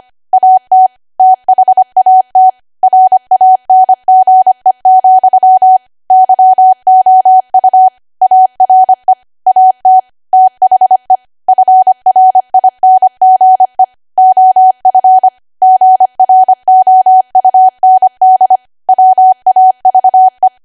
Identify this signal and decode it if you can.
Morse code, heard as a keyed tone, about 25 wpm, 750 Hz. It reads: AT THAT RANGE, YOU ARE AT THE FRINGE OF GROUND WAVE
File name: Archive / 25 WPM / 25WPM